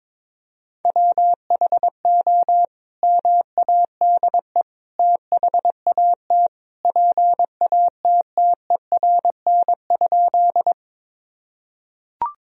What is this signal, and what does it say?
Morse code, heard as a keyed tone, about 22 wpm, 700 Hz. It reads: WHO MADE THAT PATTERN?
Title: Sentences from Top 1000 Words - Review Part 2 - 22wpm